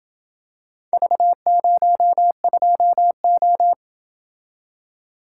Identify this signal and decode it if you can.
Morse code, heard as a keyed tone, about 27 wpm, 700 Hz. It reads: V02O